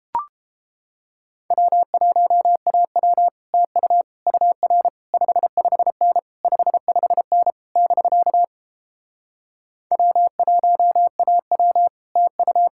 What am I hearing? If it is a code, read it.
Morse code, heard as a keyed tone, about 31 wpm, 700 Hz. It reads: W1AW TU UR 55N 55N <BK> W1AW TU